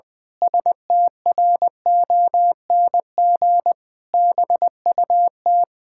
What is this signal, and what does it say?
Morse code, heard as a keyed tone, about 20 wpm, 700 Hz. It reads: STRONG BUT
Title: Pangram Challenge v2 - 20wpm